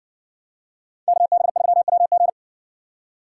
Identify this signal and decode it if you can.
Morse code, heard as a keyed tone, about 60 wpm, 700 Hz. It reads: KB3WG